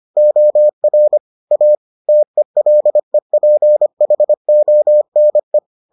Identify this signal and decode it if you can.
Morse code, heard as a keyed tone, about 25 wpm, 600 Hz. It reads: OR A TELEPHONE